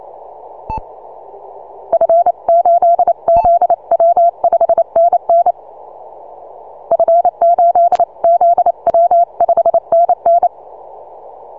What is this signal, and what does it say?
Morse code, heard as a keyed tone, about 28 wpm, 680 Hz. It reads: F8ZW 5NN F8ZW 5NN
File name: soundclips / TZ6BB 6MCW